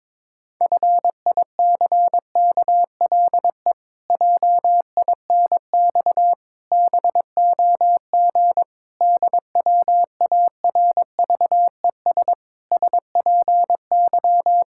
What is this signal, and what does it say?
Morse code, heard as a keyed tone, about 22 wpm, 700 Hz. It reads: FICKLE JINX BOG DWARVES SPY
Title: Pangram Challenge v2 - 22wpm